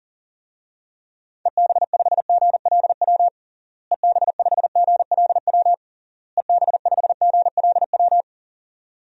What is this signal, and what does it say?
Morse code, heard as a keyed tone, about 40 wpm, 700 Hz. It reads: EB5GLW EB5GLW EB5GLW
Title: DXCC-100 Call Signs - 6 Characters - Repeated 3x - 40wpm